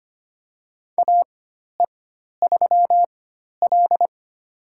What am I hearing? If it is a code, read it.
Morse code, heard as a keyed tone, about 25 wpm, spaced as for 12 wpm, 700 Hz. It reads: AE3L
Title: USA Call Signs - 4x Character Spacing - 25wpm